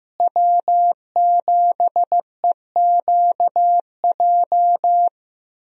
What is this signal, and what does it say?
Morse code, heard as a keyed tone, about 15 wpm, 700 Hz. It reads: W7EQJ
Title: USA Call Signs - 5 Characters - Repeated 2x - 15wpm